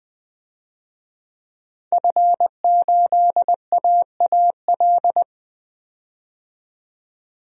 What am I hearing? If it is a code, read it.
Morse code, heard as a keyed tone, about 20 wpm, 700 Hz. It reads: F8AAL
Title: DXCC-100 Call Signs - 5 Characters - 20wpm